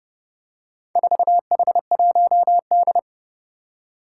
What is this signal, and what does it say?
Morse code, heard as a keyed tone, about 30 wpm, 690 Hz. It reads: 4H1D
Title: Four Challenging Letter-Number Combinations 30wpm